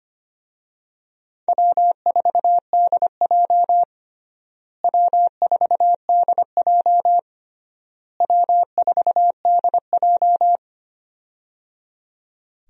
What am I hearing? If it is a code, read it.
Morse code, heard as a keyed tone, about 25 wpm, 700 Hz. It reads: W4DJ W4DJ W4DJ
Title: USA Call Signs - 4 Characters - Repeated 3x - 25wpm